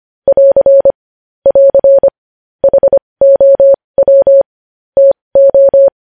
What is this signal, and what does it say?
Morse code, heard as a keyed tone, about 25 wpm, spaced as for 20 wpm, 550 Hz. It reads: <AR> <AR> HOW TO
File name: podcast / News headlines-1x-25x20